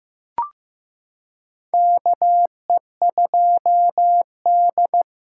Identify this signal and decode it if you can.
Morse code, heard as a keyed tone, about 15 wpm, 700 Hz. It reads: KE2D
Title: K1USN SST QSO Practice - 15wpm